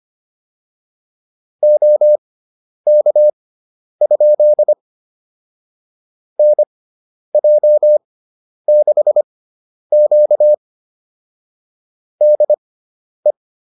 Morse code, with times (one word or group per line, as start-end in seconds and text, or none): 1.6-4.7 OK?
6.4-10.5 NJ6Q
12.2-13.3 DE